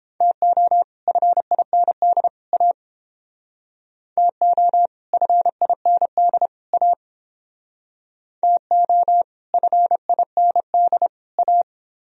Morse code, heard as a keyed tone, about 30 wpm, 700 Hz. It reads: TO FIND A TO FIND A TO FIND A